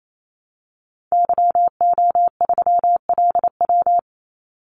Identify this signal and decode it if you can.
Morse code, heard as a keyed tone, about 28 wpm, 700 Hz. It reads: YO3LW
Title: Call Signs Repeated 2x - 28wpm